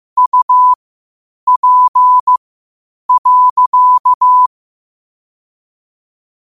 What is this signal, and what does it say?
Morse code, heard as a keyed tone, about 15 wpm, spaced as for 8 wpm, 1000 Hz. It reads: UP.
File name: qotdfiles / 8wpm